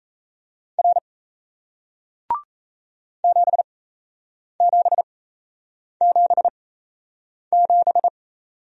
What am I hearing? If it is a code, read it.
Morse code, heard as a keyed tone, about 38 wpm, spaced as for 8 wpm, 700 Hz. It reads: R 7777